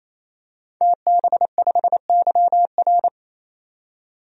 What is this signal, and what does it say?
Morse code, heard as a keyed tone, about 27 wpm, 700 Hz. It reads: TB5YR
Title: DXCC-100 Call Signs - 5 Characters - 28wpm